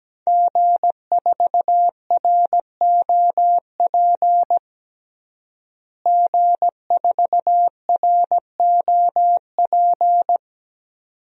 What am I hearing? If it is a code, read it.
Morse code, heard as a keyed tone, about 17 wpm, 700 Hz. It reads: G4ROP G4ROP
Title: DXCC-100 Call Signs - 5 Characters - Repeated 2x - 17wpm